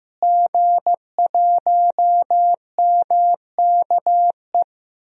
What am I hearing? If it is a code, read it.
Morse code, heard as a keyed tone, about 15 wpm, 700 Hz. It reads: G1MKE
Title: DXCC-100 Call Signs - 5 Characters - Repeated 3x - 15wpm